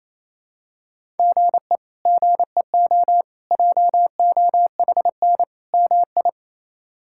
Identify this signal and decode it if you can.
Morse code, heard as a keyed tone, about 28 wpm, 700 Hz. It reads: GE GEO JOHN MI
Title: K1USN SST QSO Practice - 28wpm